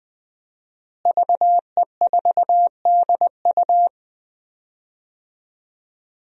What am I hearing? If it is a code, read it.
Morse code, heard as a keyed tone, about 20 wpm, 700 Hz. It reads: VE4DU
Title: DXCC-100 Call Signs - 5 Characters - Repeated 2x - 20wpm